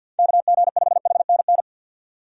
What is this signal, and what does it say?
Morse code, heard as a keyed tone, about 52 wpm, 700 Hz. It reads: KG5HNN
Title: USA Call Signs - 6 Characters - 50wpm